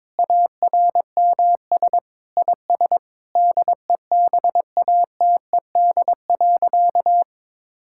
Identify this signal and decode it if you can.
Morse code, heard as a keyed tone, about 22 wpm, 700 Hz. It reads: ARMS IS DEBATED.
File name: Sentences with 7-Word Phrases - 22wpm